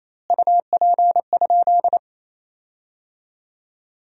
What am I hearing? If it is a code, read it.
Morse code, heard as a keyed tone, about 28 wpm, 700 Hz. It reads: UP?
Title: 2x Word-Spacing - Sentences from Top 200 Words - 28wpm